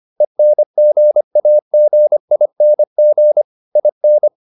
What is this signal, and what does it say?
Morse code, heard as a keyed tone, about 25 wpm, 600 Hz. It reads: ENGAGING IN